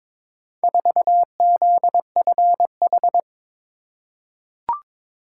Morse code, heard as a keyed tone, about 22 wpm, 700 Hz. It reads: 4ZFH
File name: practice / 22wpm / Four Letter-Number Combinations - 22wpm